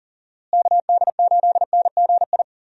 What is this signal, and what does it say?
Morse code, heard as a keyed tone, about 40 wpm, 700 Hz. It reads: KD8NGI
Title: USA Call Signs - 6 Characters - Repeated 2x - 40wpm